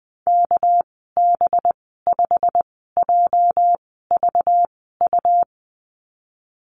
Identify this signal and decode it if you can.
Morse code, heard as a keyed tone, about 20 wpm, spaced as for 14 wpm, 700 Hz. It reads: KB5JVU
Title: 245 Lesson 39g - Letter z - Call Signs 20wpm